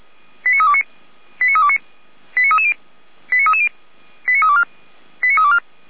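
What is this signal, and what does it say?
Five-tone selective call (ZVEI), heard as quick runs of five tones, tones of about 80 ms. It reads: 89329 89329 89330 89330 89324 89324
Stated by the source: ZVEI five-tone selective call received via SDR.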